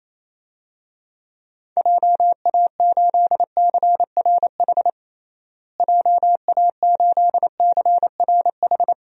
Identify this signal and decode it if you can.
Morse code, heard as a keyed tone, about 28 wpm, 700 Hz. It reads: JA8CRH JA8CRH
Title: DXCC-100 Call Signs - 6 Characters - Repeated 3x - 28wpm